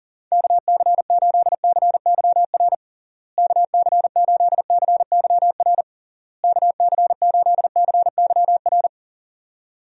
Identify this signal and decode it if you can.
Morse code, heard as a keyed tone, about 37 wpm, 700 Hz. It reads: KC8CYR KC8CYR KC8CYR
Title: USA Call Signs - 6 Characters - Repeated 3x - 40wpm